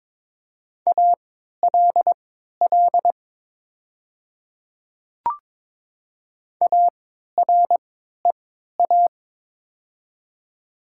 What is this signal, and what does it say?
Morse code, heard as a keyed tone, about 22 wpm, spaced as for 13 wpm, 700 Hz. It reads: ALL AREA
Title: 3x Character Spacing - Top 500 Words - 22wpm